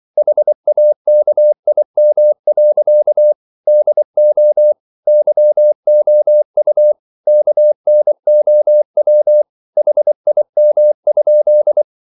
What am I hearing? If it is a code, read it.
Morse code, heard as a keyed tone, about 24 wpm, 600 Hz. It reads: HAKIM. DO YOU KNOW HIM?